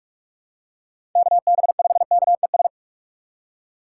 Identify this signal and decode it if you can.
Morse code, heard as a keyed tone, about 44 wpm, 700 Hz. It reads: KB5KES